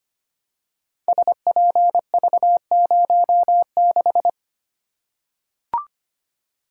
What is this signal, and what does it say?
Morse code, heard as a keyed tone, about 25 wpm, 700 Hz. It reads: SPV06
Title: Code Groups - 5 Letter:Numbers - 25wpm